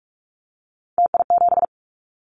Morse code, heard as a keyed tone, about 46 wpm, 700 Hz. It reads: TI7
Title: Callsign Prefixes for Top 100 DXCC - 45wpm